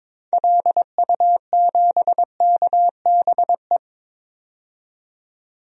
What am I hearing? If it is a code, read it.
Morse code, heard as a keyed tone, about 22 wpm, 700 Hz. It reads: LU7KBE
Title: DXCC-100 Call Signs - 6 Characters - 22wpm